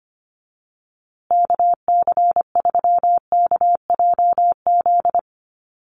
Morse code, heard as a keyed tone, about 25 wpm, 700 Hz. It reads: KC3KJZ
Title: USA Call Signs Repeated 2x 25wpm